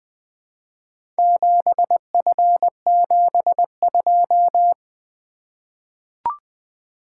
Morse code, heard as a keyed tone, about 20 wpm, 700 Hz. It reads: 7F72
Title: Four Letter-Number Combinations - 20wpm